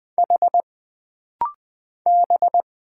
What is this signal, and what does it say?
Morse code, heard as a keyed tone, about 20 wpm, 700 Hz. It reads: H B
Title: Single Letter-Number Alt - Rapid-Fire 20wpm